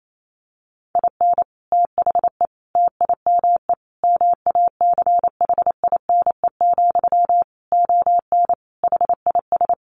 Morse code, heard as a keyed tone, about 28 wpm, 700 Hz. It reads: IN THE TIME MACHINE, ON HIS
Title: A Sound Of Thunder 28wpm